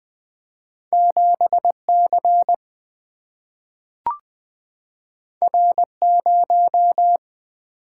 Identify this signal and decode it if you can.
Morse code, heard as a keyed tone, about 20 wpm, 700 Hz. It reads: 7C R0
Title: Single-Letter Single-Number 20wpm